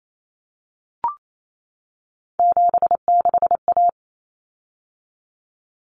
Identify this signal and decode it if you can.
Morse code, heard as a keyed tone, about 28 wpm, 700 Hz. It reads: 76A
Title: Common Cut Numbers - 3-Digit 28wpm